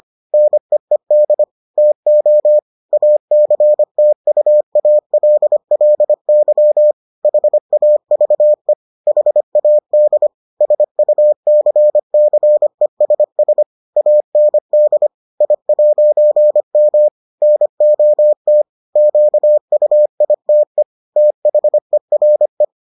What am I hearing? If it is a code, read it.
Morse code, heard as a keyed tone, about 25 wpm, 600 Hz. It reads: NEED TO ACTUALLY HAVE HAD SUCCESS AND I'M NOT QUITE THERE